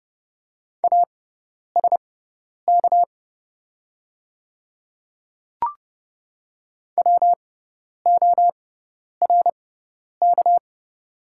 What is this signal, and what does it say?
Morse code, heard as a keyed tone, about 30 wpm, spaced as for 10 wpm, 700 Hz. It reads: ASK WORK
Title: 6x Character Spacing - Top 500 Words - 30wpm